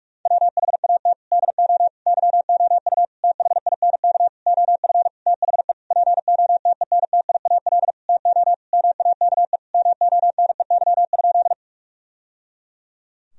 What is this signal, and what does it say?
Morse code, heard as a keyed tone, about 41 wpm, 700 Hz. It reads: WHAT DO YOU THINK OF THE POTENTIAL TO MAKE MONEY?
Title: Sentences from Top 1000 Words - Review Part 4 - 45wpm